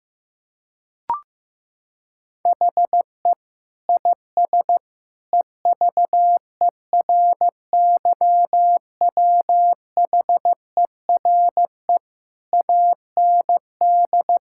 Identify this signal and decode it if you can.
Morse code, heard as a keyed tone, about 15 wpm, 700 Hz. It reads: HE IS EVERYWHERE AND
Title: Sentences from Top 1600 Words 15wpm